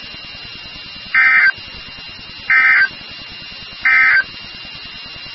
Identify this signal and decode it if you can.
Emergency Alert System (SAME) tones, heard as harsh buzzing bursts. It is heard from 1.0 to 4.3 s.
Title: NOAA Weather Radio WXK36